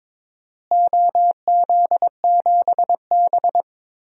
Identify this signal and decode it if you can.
Morse code, heard as a keyed tone, about 22 wpm, 700 Hz. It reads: OZ7B
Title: DXCC-100 Call Signs - 4 Characters - Repeated 3x - 22wpm